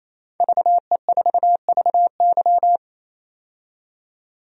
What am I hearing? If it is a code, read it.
Morse code, heard as a keyed tone, about 27 wpm, 700 Hz. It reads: VE4VY
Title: DXCC-100 Call Signs - 5 Characters - 28wpm